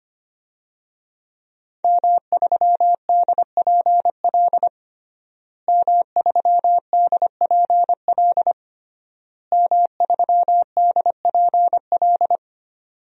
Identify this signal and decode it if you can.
Morse code, heard as a keyed tone, about 25 wpm, 700 Hz. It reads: M3DPL M3DPL M3DPL
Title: DXCC-100 Call Signs - 5 Characters - Repeated 3x - 25wpm